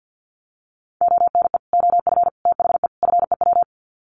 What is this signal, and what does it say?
Morse code, heard as a keyed tone, about 50 wpm, 700 Hz. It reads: ONE OF THE FEW
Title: Sets of 4 Words 50wpm